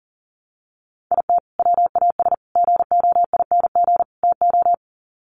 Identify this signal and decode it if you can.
Morse code, heard as a keyed tone, about 40 wpm, 700 Hz. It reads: IT WAS GOING TO
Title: Sets of 4 Words 40wpm